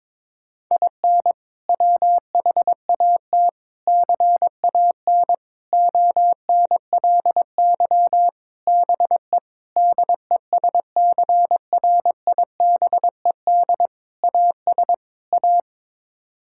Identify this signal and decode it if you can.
Morse code, heard as a keyed tone, about 22 wpm, 700 Hz. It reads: IN WHAT CAN ONLY BE DESCRIBED AS A
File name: practice / 22wpm / Sets of 8 Words - 22wpm